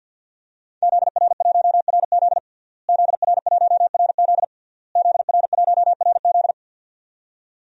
Morse code, heard as a keyed tone, about 49 wpm, 700 Hz. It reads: ZR1RZ ZR1RZ ZR1RZ
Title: DXCC-100 Call Signs - 5 Characters - Repeated 3x - 50wpm